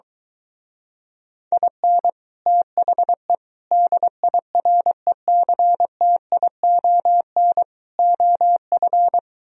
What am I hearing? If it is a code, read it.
Morse code, heard as a keyed tone, about 23 wpm, 700 Hz. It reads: IN THE DIRECTION OF